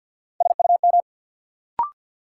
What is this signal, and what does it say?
Morse code, heard as a keyed tone, about 53 wpm, 700 Hz. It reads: SUM
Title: Sentences from Top 1000 Words - Review Part 6 - New Word Drill - 50wpm